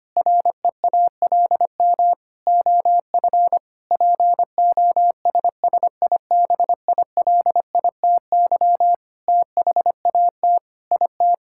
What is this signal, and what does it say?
Morse code, heard as a keyed tone, about 25 wpm, 700 Hz. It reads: REALM OF POSSIBILITY THAT IT
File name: practice / 25wpm / Sentences with 9-Word Phrases - 25wpm